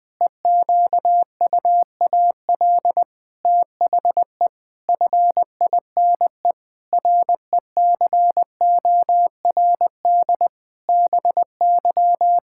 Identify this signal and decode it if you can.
Morse code, heard as a keyed tone, about 20 wpm, 700 Hz. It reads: EQUAL THE FINE RECORD BY
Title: Pangram Challenge v2 - 20wpm